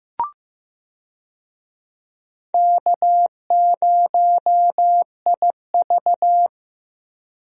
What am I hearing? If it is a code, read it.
Morse code, heard as a keyed tone, about 15 wpm, 700 Hz. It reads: K0IV
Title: USA Call Signs - 4 Characters - Repeated 2x - 15wpm